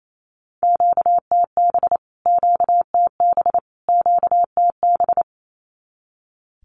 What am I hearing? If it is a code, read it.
Morse code, heard as a keyed tone, about 28 wpm, 700 Hz. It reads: QTB QTB QTB
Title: Q-Codes Repeated 3x 28wpm